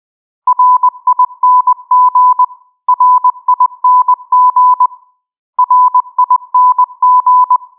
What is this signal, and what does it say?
Morse code, heard as a keyed tone, about 20 wpm, 1000 Hz. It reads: RING RING RING